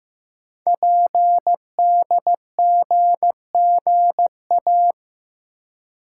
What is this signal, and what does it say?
Morse code, heard as a keyed tone, about 15 wpm, 700 Hz. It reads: PDGGA
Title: Code Groups - 5 Letters - 15wpm